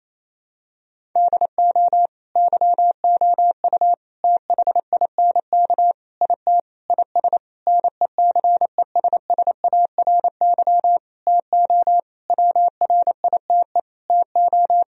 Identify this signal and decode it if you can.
Morse code, heard as a keyed tone, about 28 wpm, 700 Hz. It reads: DO YOU THINK IT IS NECESSARY TO WRITE TO
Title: Sentences from Top 1000 Words - Review Part 1 - 28wpm